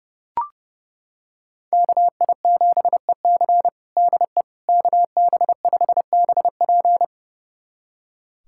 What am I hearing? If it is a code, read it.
Morse code, heard as a keyed tone, about 29 wpm, 700 Hz. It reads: KI7EC DE KB5BP
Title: USA Callsign DE Callsign - 5 Characters - 30wpm